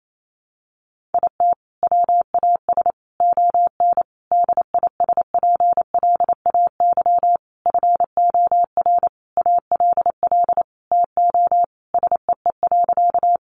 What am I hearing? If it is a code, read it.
Morse code, heard as a keyed tone, about 28 wpm, 700 Hz. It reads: IT WAS ON DISPLAY FOR ALL TO SEE.
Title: Sentences from Top 1200 Words 28wpm